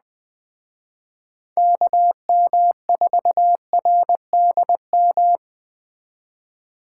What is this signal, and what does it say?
Morse code, heard as a keyed tone, about 20 wpm, 700 Hz. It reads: KM4RDM
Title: Speed Racing - USA 2x3 Call Signs 17wpm